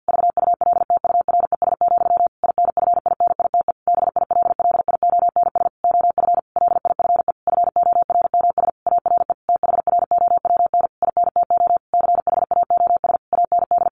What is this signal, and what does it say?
Morse code, heard as a keyed tone, about 50 wpm, 700 Hz. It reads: VULTURES, INFINITE BILLIONS OF LIFE FORMS ARE THROWN INTO CHAOS AND
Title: A Sound Of Thunder 50wpm